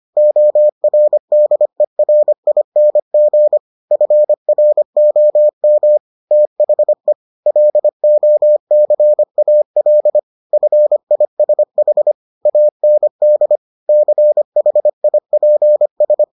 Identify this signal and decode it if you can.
Morse code, heard as a keyed tone, about 25 wpm, 600 Hz. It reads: ORDERING FROM THE LOCAL FISH AND CHIPS